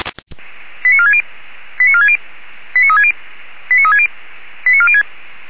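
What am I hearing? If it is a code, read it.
Five-tone selective call (ZVEI), heard as quick runs of five tones, tones of about 70 ms. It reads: 89460 89460 89360 89360 89406
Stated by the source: Five-tone ZVEI selective call received by SDR, from the audio server of Feuerwehr Orlamünde.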